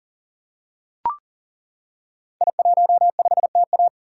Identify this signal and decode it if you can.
Morse code, heard as a keyed tone, about 40 wpm, 700 Hz. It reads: I15TA
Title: Code Groups - 5 Letter:Numbers - 40wpm